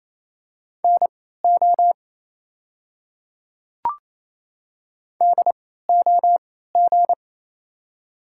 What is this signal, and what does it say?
Morse code, heard as a keyed tone, about 28 wpm, spaced as for 16 wpm, 700 Hz. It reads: NO DOG
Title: 3x Character Spacing - Top 500 Words - 28wpm